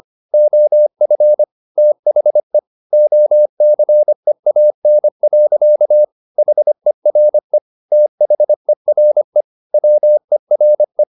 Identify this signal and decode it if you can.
Morse code, heard as a keyed tone, about 25 wpm, 600 Hz. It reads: OF THE OCEAN. HERE THERE WERE
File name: the old man and the sea part 1 25wpm